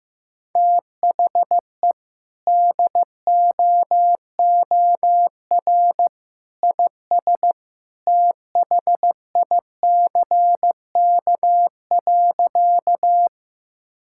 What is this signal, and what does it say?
Morse code, heard as a keyed tone, about 16 wpm, 700 Hz. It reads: THE DOOR IS THICK.
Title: Sentences from Top 1600 Words 15wpm